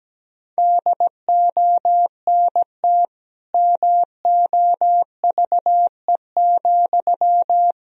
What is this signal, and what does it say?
Morse code, heard as a keyed tone, about 17 wpm, 700 Hz. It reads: DONT MOVE,
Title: Sentences from Top 300 Words and Abbreviations 17wpm